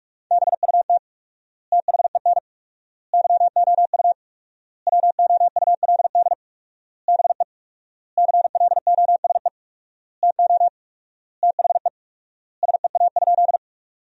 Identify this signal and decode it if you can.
Morse code, heard as a keyed tone, about 42 wpm, 700 Hz. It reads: BUT THEN YOU WOULD BE CLOSE TO THE SEA?
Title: 4x Word-Spacing - Sentences from Top 300 Words - 45wpm